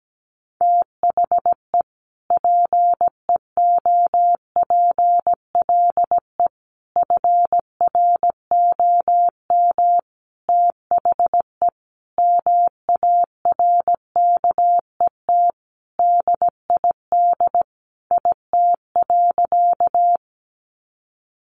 Morse code, heard as a keyed tone, about 17 wpm, 700 Hz. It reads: THE PEOPLE FROM THE MARKET DID IT.
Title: Sentences from Top 600 Words 17wpm